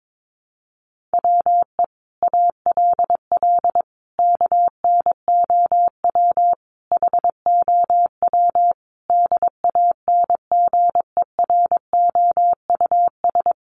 Morse code, heard as a keyed tone, about 22 wpm, 700 Hz. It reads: WE ALL KNOW HOW DANGEROUS
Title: Sentences from Top 1200 Words 22wpm